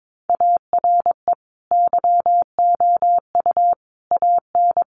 Morse code, heard as a keyed tone, about 22 wpm, 700 Hz. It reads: ARE YOU AN
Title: Sentences from Top 1200 Words 22wpm